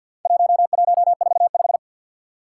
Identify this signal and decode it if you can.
Morse code, heard as a keyed tone, about 50 wpm, 700 Hz. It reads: 1145
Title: Four Challenging Letter-Number Combinations 50wpm